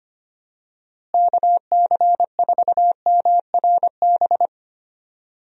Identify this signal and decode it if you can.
Morse code, heard as a keyed tone, about 25 wpm, 700 Hz. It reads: KC4MRB
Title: USA Call Signs - 6 Characters - Repeated 2x - 25wpm